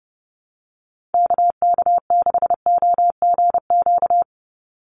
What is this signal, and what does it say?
Morse code, heard as a keyed tone, about 30 wpm, 700 Hz. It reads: KK6OGQ
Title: USA Call Signs 30wpm